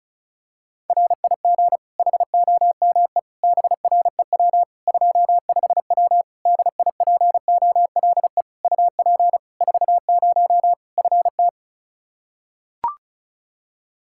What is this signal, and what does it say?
Morse code, heard as a keyed tone, about 34 wpm, 700 Hz. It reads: RIG HOME BREW 25W DIPOLE UP 40 FT
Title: K1USN SST QSO Practice 35wpm